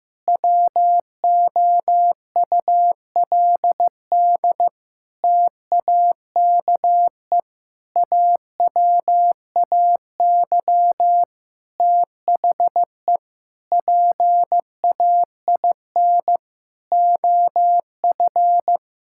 Morse code, heard as a keyed tone, about 15 wpm, 700 Hz. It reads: WOULD TAKE AWAY THE PAIN OF
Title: Sentences from Top 1600 Words 15wpm